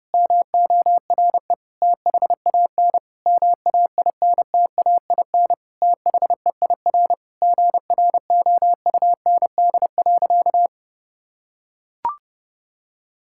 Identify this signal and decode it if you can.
Morse code, heard as a keyed tone, about 29 wpm, 700 Hz. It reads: MORE THAN MAINTAIN THEIR GROUND.
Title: Sentences from Top 1000 Words - Review Part 5 - 30wpm